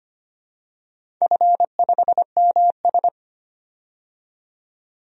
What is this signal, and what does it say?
Morse code, heard as a keyed tone, about 25 wpm, 700 Hz. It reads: F5MS
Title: DXCC-100 Call Signs - 4 Characters - 25wpm